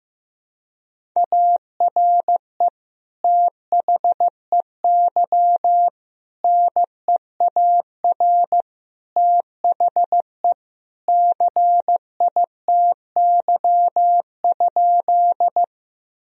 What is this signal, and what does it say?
Morse code, heard as a keyed tone, about 15 wpm, 700 Hz. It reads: ARE THEY NEAR THE CITY?
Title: Sentences from Top 1000 Words - Review Part 1 - 15wpm